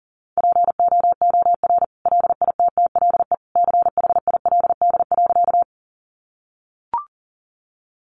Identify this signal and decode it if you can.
Morse code, heard as a keyed tone, about 40 wpm, 700 Hz. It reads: POOR LITTLE CHILD.
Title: Sentences from Top 1000 Words 40wpm